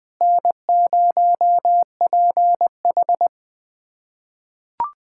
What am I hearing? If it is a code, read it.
Morse code, heard as a keyed tone, about 20 wpm, 700 Hz. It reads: N0PH
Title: USA Callsign DE Callsign - 4 Characters - 20wpm